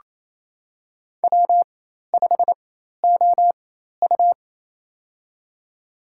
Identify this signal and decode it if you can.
Morse code, heard as a keyed tone, about 28 wpm, spaced as for 13 wpm, 700 Hz. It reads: W5OU
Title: USA Call Signs - 4 Characters - 4x Character Spacing - 28wpm